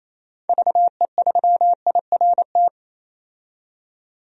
Morse code, heard as a keyed tone, about 28 wpm, 700 Hz. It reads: VE3IRT
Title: DXCC-100 Call Signs - 6 Characters - 28wpm